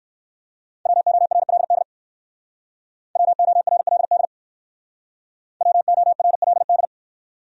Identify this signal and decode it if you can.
Morse code, heard as a keyed tone, about 57 wpm, 700 Hz. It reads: WORLD WORLD WORLD